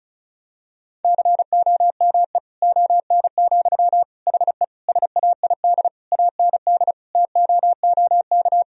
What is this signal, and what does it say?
Morse code, heard as a keyed tone, about 34 wpm, 700 Hz. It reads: COME ON, HE SAID AND TOOK
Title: Sentences from Top 1000 Words - Review Part 2 - 35wpm